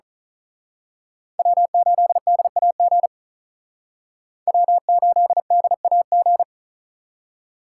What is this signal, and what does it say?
Morse code, heard as a keyed tone, about 39 wpm, 700 Hz. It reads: W8DAG W8DAG
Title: Speed Racing - USA 1x3 Call Signs 35wpm